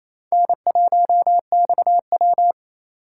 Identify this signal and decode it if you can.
Morse code, heard as a keyed tone, about 28 wpm, 700 Hz. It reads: N1XW
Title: USA Call Signs - 4 Characters - 4x Character Spacing - 28wpm